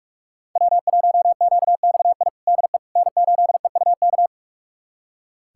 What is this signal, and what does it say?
Morse code, heard as a keyed tone, about 46 wpm, 700 Hz. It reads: W1QXI DE N8EUK
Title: USA Callsign DE Callsign - 5 Characters - 45wpm